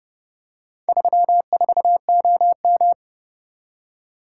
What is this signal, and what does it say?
Morse code, heard as a keyed tone, about 28 wpm, 700 Hz. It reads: 34OM